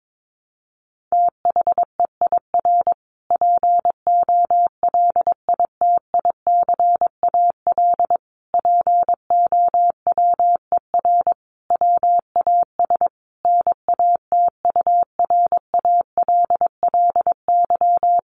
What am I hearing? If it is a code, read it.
Morse code, heard as a keyed tone, about 22 wpm, 700 Hz. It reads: THEIR POLITICAL POWER WAS NATURALLY
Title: Sentences from Top 1300 Words 22wpm